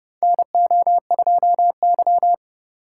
Morse code, heard as a keyed tone, about 30 wpm, 700 Hz. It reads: NO2Y
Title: USA Call Signs - 2x Character Spacing - 30wpm